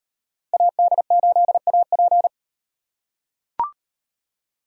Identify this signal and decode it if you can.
Morse code, heard as a keyed tone, about 38 wpm, 700 Hz. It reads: AD8AP